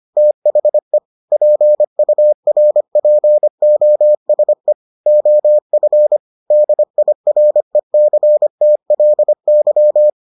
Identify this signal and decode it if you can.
Morse code, heard as a keyed tone, about 25 wpm, 600 Hz. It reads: THE PURPOSE OF DIRECTLY